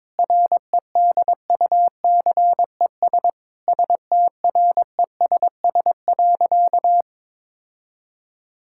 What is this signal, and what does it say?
Morse code, heard as a keyed tone, about 22 wpm, 700 Hz. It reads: REDUCES STRESS.
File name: Difficult Sentences with Hints - 22wpm